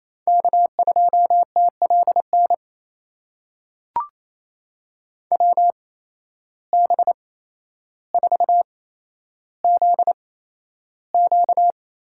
Morse code, heard as a keyed tone, about 28 wpm, spaced as for 12 wpm, 700 Hz. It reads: K2TLN WB4ZQ
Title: USA Call Signs - 8x Character Spacing - 28wpm